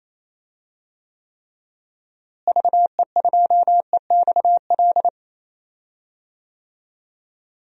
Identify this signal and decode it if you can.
Morse code, heard as a keyed tone, about 28 wpm, 700 Hz. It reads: VE2EXL
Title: DXCC-100 Call Signs - 6 Characters - Repeated 3x - 28wpm